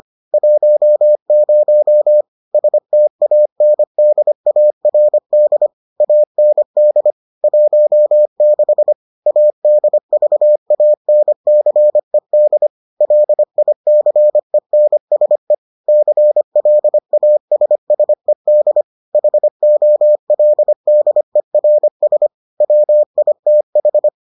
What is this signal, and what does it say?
Morse code, heard as a keyed tone, about 25 wpm, 600 Hz. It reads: 10 STANDARD AND 16 ADVANCED LICENSE CLASSED HOLDERS WITH